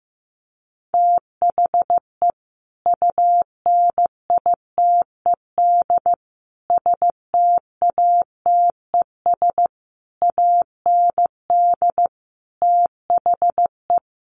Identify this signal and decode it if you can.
Morse code, heard as a keyed tone, about 15 wpm, 700 Hz. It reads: THE UNITED STATES AND THE
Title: Sets of 5 Words 15wpm